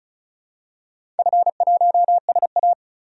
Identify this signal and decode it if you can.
Morse code, heard as a keyed tone, about 35 wpm, 700 Hz. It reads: F1SA
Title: DXCC-100 Call Signs - 4 Characters - 35wpm